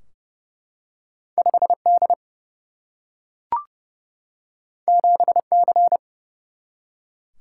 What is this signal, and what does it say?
Morse code, heard as a keyed tone, about 29 wpm, 700 Hz. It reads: 5 D 7 C
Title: Single-Letter Single-Number 30wpm